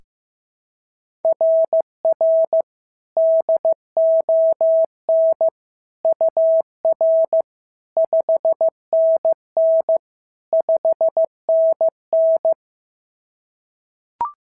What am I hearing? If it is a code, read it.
Morse code, heard as a keyed tone, about 14 wpm, 650 Hz. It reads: RR DON UR 5NN 5NN
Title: SOTA QSO Practice 15wpm